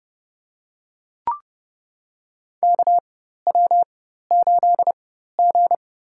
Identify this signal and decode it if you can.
Morse code, heard as a keyed tone, about 30 wpm, spaced as for 14 wpm, 690 Hz. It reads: KW8G
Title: USA Call Signs - 4x Character Spacing - 30wpm